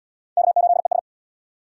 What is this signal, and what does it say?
Morse code, heard as a keyed tone, about 66 wpm, 700 Hz. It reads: K8ES